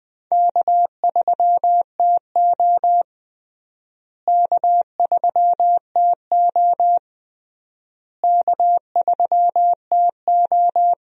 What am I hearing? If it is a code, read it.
Morse code, heard as a keyed tone, about 20 wpm, 700 Hz. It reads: K3TO K3TO K3TO
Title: USA Call Signs - 4 Characters - Repeated 3x - 20wpm